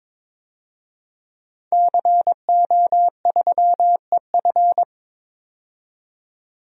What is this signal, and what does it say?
Morse code, heard as a keyed tone, about 21 wpm, 700 Hz. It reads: CO3EF
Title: DXCC-100 Call Signs - 5 Characters - 22wpm